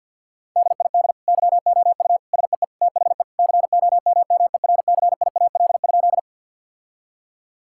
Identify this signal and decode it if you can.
Morse code, heard as a keyed tone, about 50 wpm, 700 Hz. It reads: DID YOU SEE THE COMMERCIAL?
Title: Sentences from Top 1000 Words - Review Part 5 - 50wpm